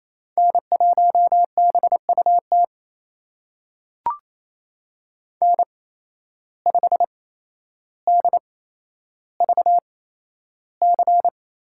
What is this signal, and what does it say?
Morse code, heard as a keyed tone, about 27 wpm, spaced as for 12 wpm, 700 Hz. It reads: N1BUT N5DVC
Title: USA Call Signs - 5 Characters - 8x Character Spacing - 28wpm